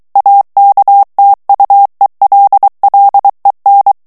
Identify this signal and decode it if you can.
Morse code, heard as a keyed tone, about 23 wpm, 800 Hz. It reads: AKTUELLEN
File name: ALLEMAND texte clair 1 vitesse 22 mots minute